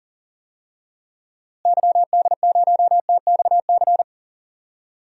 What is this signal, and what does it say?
Morse code, heard as a keyed tone, about 39 wpm, 700 Hz. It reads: YD0TXC
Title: DXCC-100 Call Signs - 6 Characters - Repeated 3x - 40wpm